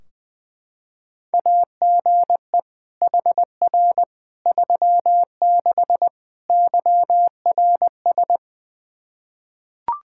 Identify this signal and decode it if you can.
Morse code, heard as a keyed tone, about 20 wpm, 700 Hz. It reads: AGE HR 36 YRS
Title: Practice QSO - Exchange 3 - More Info - 20wpm